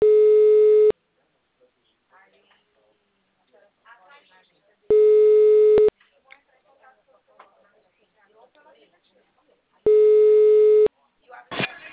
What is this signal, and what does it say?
Ringback tone (Europe, 425 Hz), with speech.
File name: out- -2011